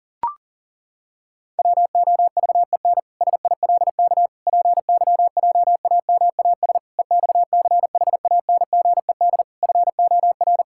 Morse code, heard as a keyed tone, about 40 wpm, 700 Hz. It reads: WOVEN SILK PYJAMAS EXCHANGED FOR
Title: Pangram Challenge v2 - 40wpm